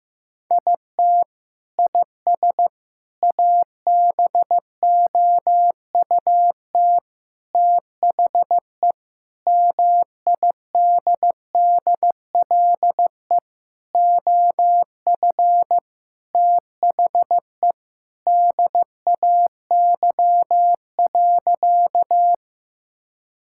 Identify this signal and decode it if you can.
Morse code, heard as a keyed tone, about 15 wpm, 700 Hz. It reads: IT IS ABOUT THE MIDDLE OF THE DAY.
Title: Sentences from Top 1000 Words - Review Part 6 - 15wpm